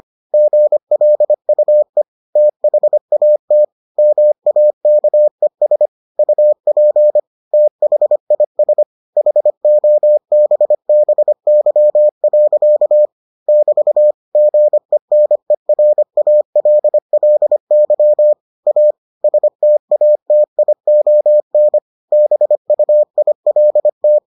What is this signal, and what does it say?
Morse code, heard as a keyed tone, about 25 wpm, 600 Hz. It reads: GLUE THAT MAKES UP THIS HOBBY. <BT> GENERALLY A STATION BUILT